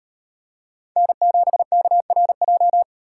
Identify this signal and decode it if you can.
Morse code, heard as a keyed tone, about 38 wpm, 700 Hz. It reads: N7KRJ